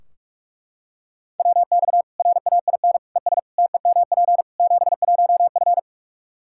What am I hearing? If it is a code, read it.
Morse code, heard as a keyed tone, about 42 wpm, 700 Hz. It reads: WX RAIN ES TEMP 71F
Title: K1USN SST QSO Practice 45wpm